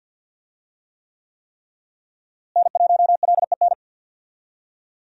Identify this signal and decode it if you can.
Morse code, heard as a keyed tone, about 52 wpm, 700 Hz. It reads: N1LEN
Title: USA Call Signs - 5 Characters - Repeated 3x - 50wpm